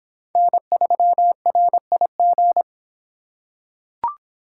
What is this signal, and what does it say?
Morse code, heard as a keyed tone, about 25 wpm, 700 Hz. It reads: N3RIG